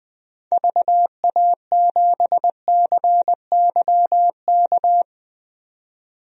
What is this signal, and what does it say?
Morse code, heard as a keyed tone, about 20 wpm, 700 Hz. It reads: VA7CYK
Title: DXCC-100 Call Signs - 6 Characters - 20wpm